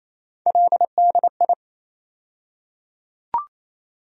Morse code, heard as a keyed tone, about 28 wpm, 700 Hz. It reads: LDI
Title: USA Call Sign Postfixes - 3 Letters 28wpm